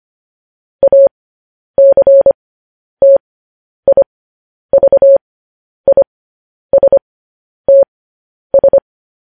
Morse code, heard as a keyed tone, about 25 wpm, spaced as for 10 wpm, 550 Hz. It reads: ACTIVISTS
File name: podcast / News headlines-1x-25x10